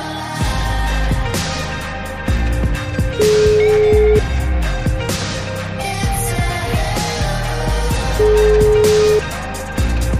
Ringback tone (Europe, 425 Hz), with noise.